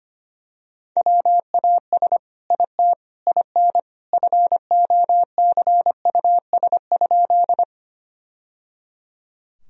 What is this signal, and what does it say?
Morse code, heard as a keyed tone, about 25 wpm, 700 Hz. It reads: WAS IT IN FOCUS?
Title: Sentences from Top 1000 Words - Review Part 4 - 25wpm